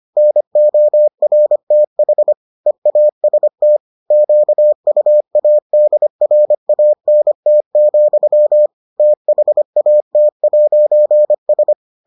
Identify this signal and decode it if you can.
Morse code, heard as a keyed tone, about 25 wpm, 600 Hz. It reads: NORTH EAST QUADRANT, THAT'S